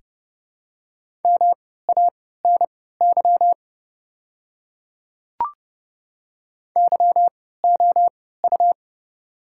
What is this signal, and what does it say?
Morse code, heard as a keyed tone, about 30 wpm, spaced as for 17 wpm, 700 Hz. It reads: MANY YOU
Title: 3x Character Spacing - Top 100 Words - 30wpm